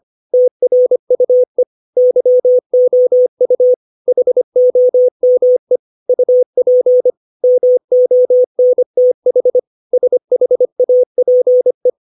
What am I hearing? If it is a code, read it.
Morse code, heard as a keyed tone, about 25 wpm, 500 Hz. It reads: TRUE YOU HOME UP MONTH SHAPE
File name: 632 words / 632 Words 25WPM